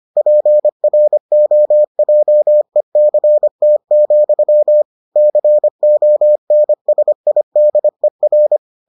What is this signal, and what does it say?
Morse code, heard as a keyed tone, about 25 wpm, 600 Hz. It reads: PROJECT, CONSIDER